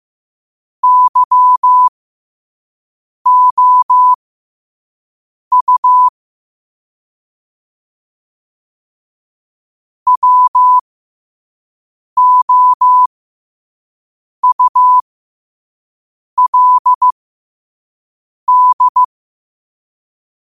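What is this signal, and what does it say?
Morse code, heard as a keyed tone, about 15 wpm, spaced as for 5 wpm, 1000 Hz. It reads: YOU WOULD
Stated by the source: Morse practice quote sent at 5 wpm.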